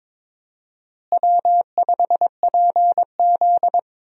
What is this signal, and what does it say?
Morse code, heard as a keyed tone, about 22 wpm, 700 Hz. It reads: W5PZ
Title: USA Call Signs - 4 Characters - Repeated 3x - 22wpm